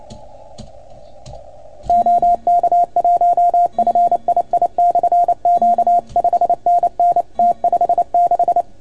Morse code, heard as a keyed tone, about 28 wpm, 690 Hz. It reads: OK1FII/Q 5NN T56
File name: Archiv / CUC310 / OK1FGD CUC 310 pileup